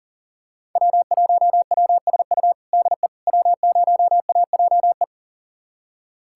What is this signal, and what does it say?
Morse code, heard as a keyed tone, about 40 wpm, 700 Hz. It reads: W1WSU DE W0AJE
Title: USA Callsign DE Callsign - 5 Characters - 40wpm